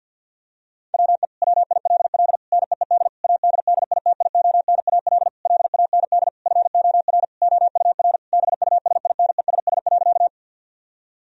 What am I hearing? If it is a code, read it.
Morse code, heard as a keyed tone, about 50 wpm, 700 Hz. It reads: WE WILL NEED ADDITIONAL LAND FOR OUR BUSINESS.
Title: Sentences from Top 1000 Words - Review Part 4 - 50wpm